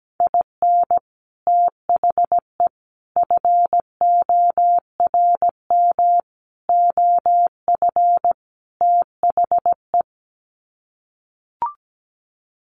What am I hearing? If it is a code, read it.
Morse code, heard as a keyed tone, about 17 wpm, 700 Hz. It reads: IN THE FORM OF THE
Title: Sets of 5 Words Encore2 17wpm